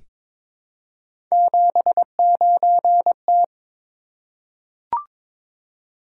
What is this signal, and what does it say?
Morse code, heard as a keyed tone, about 22 wpm, 700 Hz. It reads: 79T
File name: practice / 22wpm / Three Letter-Number Combinations - 22wpm